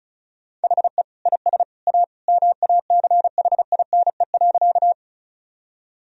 Morse code, heard as a keyed tone, about 36 wpm, 700 Hz. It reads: HE IS A MACHINE.
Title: Sentences from Top 1000 Words - Review Part 2 - 35wpm